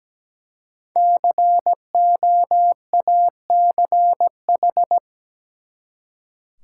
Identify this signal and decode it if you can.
Morse code, heard as a keyed tone, about 17 wpm, 700 Hz. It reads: COACH
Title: Five Letter Words - 17wpm